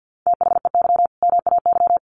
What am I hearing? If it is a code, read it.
Morse code, heard as a keyed tone, about 49 wpm, 700 Hz. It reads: THEY MAY
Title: Sentences from Top 300 Words 50wpm